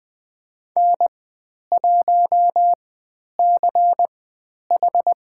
Morse code, heard as a keyed tone, about 20 wpm, spaced as for 10 wpm, 700 Hz. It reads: N1CH
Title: 095 Lesson 16g - Letter c - Call Signs 20wpm